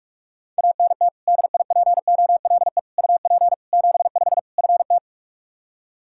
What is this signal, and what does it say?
Morse code, heard as a keyed tone, about 46 wpm, 700 Hz. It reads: ANT DIPOLE UP 75 FT
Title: K1USN SST QSO Practice 45wpm